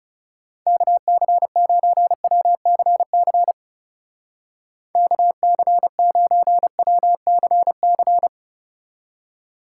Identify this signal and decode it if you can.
Morse code, heard as a keyed tone, about 37 wpm, 700 Hz. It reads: KC9WCC KC9WCC
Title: Speed Racing - USA 2x3 Call Signs 30wpm